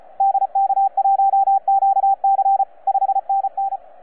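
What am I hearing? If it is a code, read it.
Morse code, heard as a keyed tone, about 33 wpm, 780 Hz, with radio noise. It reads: DK1QC 5NN
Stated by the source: FS/LJ2IJ heard on 12 metres CW, from DX World.net.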